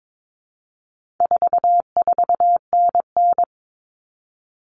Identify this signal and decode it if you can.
Morse code, heard as a keyed tone, about 22 wpm, 700 Hz. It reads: 44NN
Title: Common Cut Numbers - 4-Digit 22wpm